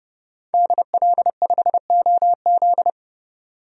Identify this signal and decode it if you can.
Morse code, heard as a keyed tone, about 30 wpm, 700 Hz. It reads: DL5OZ
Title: DXCC-100 Call Signs - 5 Characters - 30wpm